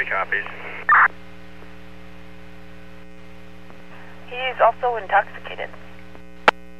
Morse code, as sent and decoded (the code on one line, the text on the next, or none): none